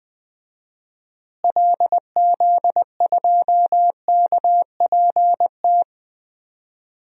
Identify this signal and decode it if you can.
Morse code, heard as a keyed tone, about 20 wpm, 700 Hz. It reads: LZ2KPT
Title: DXCC-100 Call Signs - 6 Characters - Repeated 3x - 20wpm